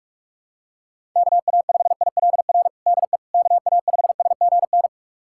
Morse code, heard as a keyed tone, about 46 wpm, 700 Hz. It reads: KA5ILR DE KA5SGN
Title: USA Callsign DE Callsign - 6 Characters - 45wpm